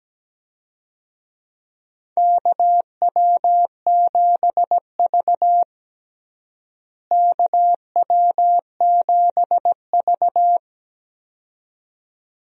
Morse code, heard as a keyed tone, about 17 wpm, 700 Hz. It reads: KW7V KW7V
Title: USA Call Signs - 4 Characters - Repeated 2x - 17wpm